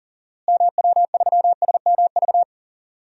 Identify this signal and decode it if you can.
Morse code, heard as a keyed tone, about 40 wpm, 700 Hz. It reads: MW3SMV
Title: DXCC-100 Call Signs - 6 Characters - Repeated 2x - 40wpm